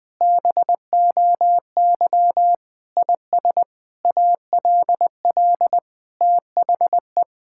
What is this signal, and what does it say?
Morse code, heard as a keyed tone, about 20 wpm, 700 Hz. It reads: BOY IS ALL THE
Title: Sentences from Top 1000 Words - Review Part 5 - 20wpm